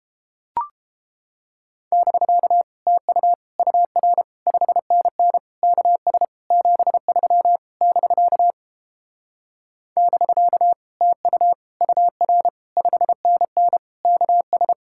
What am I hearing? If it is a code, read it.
Morse code, heard as a keyed tone, about 29 wpm, 700 Hz. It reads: <BK> TU UR 5NN KS 73 <BK> <BK> TU UR 5NN KS